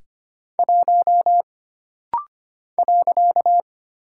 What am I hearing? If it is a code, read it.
Morse code, heard as a keyed tone, about 25 wpm, 700 Hz. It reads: 1 .
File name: Single Character (Letter:Number:Punctuation:Prosign) - Rapid-Fire - 25wpm